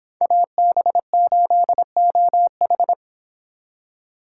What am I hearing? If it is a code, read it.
Morse code, heard as a keyed tone, about 31 wpm, 700 Hz. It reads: AB8OH